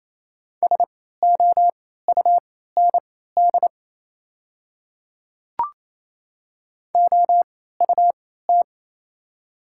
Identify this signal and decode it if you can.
Morse code, heard as a keyed tone, about 28 wpm, spaced as for 16 wpm, 690 Hz. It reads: SOUND OUT
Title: 3x Character Spacing - Top 100 Words - 28wpm